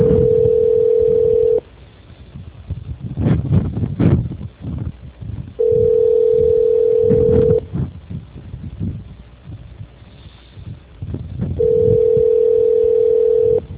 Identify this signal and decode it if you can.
Ringback tone (North America), 440 + 480 Hz.
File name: out- -2036- - - .53